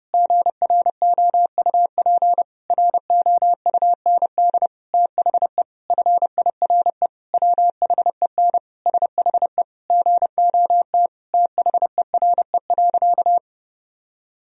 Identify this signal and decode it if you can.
Morse code, heard as a keyed tone, about 30 wpm, 700 Hz. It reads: GROUP ROUND THE FIRE WHEN SHE GOT THERE.
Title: Sentences from Top 1000 Words - Review Part 1 - 30wpm